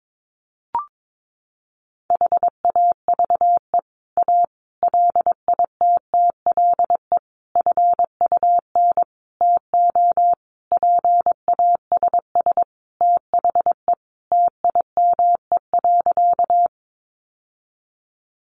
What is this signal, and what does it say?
Morse code, heard as a keyed tone, about 22 wpm, 700 Hz. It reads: HAVE A LITTLE FUN TO PASS THE TIME.
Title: Sentences from Top 700 Words 22wpm